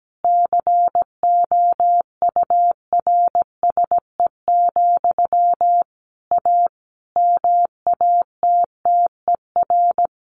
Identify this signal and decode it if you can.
Morse code, heard as a keyed tone, about 17 wpm, 700 Hz. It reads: COURSE, A MATTER
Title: Sentences from Top 1500 Words 17wpm